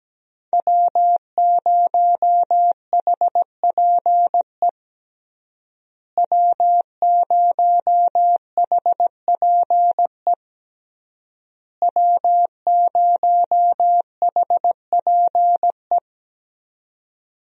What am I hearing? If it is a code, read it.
Morse code, heard as a keyed tone, about 17 wpm, 700 Hz. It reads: W0HPE W0HPE W0HPE
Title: USA Call Signs - 5 Characters - Repeated 3x - 17wpm